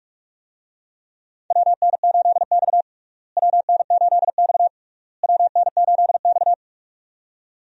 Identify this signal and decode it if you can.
Morse code, heard as a keyed tone, about 47 wpm, 700 Hz. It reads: WN8X WN8X WN8X
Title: USA Call Signs - 4 Characters - Repeated 3x - 45wpm